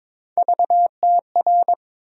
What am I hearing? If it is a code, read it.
Morse code, heard as a keyed tone, about 22 wpm, 700 Hz. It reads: VTR
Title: USA Call Sign Postfixes - 3 Letters 22wpm